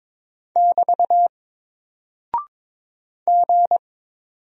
Morse code, heard as a keyed tone, about 22 wpm, 700 Hz. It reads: <BT> G
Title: Single Character (Letter:Number:Punctuation:Prosign) 22wpm